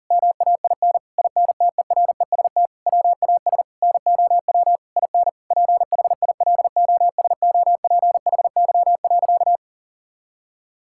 Morse code, heard as a keyed tone, about 40 wpm, 700 Hz. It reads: MAIN INTEREST WAS NOW IN PHILOSOPHY.
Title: Sentences from Top 1000 Words - Review Part 6 - 40wpm